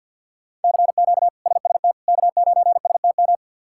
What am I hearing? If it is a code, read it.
Morse code, heard as a keyed tone, about 51 wpm, 700 Hz. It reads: CQ SST K9STM
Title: K1USN SST QSO Practice - 50wpm